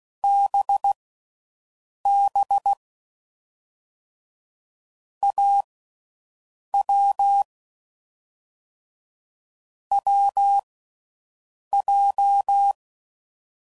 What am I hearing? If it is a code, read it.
Morse code, heard as a keyed tone, about 16 wpm, spaced as for 6 wpm, 800 Hz. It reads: BB AW WJ